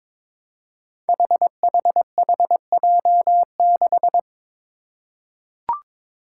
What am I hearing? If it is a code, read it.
Morse code, heard as a keyed tone, about 22 wpm, 700 Hz. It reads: HHHJ6
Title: Five Challenging Letter-Number Combinations 22wpm